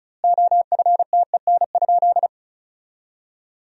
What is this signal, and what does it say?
Morse code, heard as a keyed tone, about 36 wpm, 700 Hz. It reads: OFTEN?
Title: 2x Word-Spacing - Sentences from Top 400 Words - 35wpm